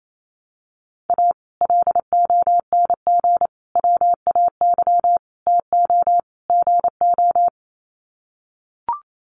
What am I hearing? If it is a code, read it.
Morse code, heard as a keyed tone, about 28 wpm, 700 Hz. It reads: A LONG WAY TO GO
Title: Sets of 5 Words 28wpm